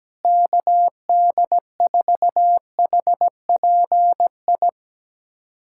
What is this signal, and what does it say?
Morse code, heard as a keyed tone, about 17 wpm, 700 Hz. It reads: KD4HPI
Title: USA Callsign DE Callsign 17wpm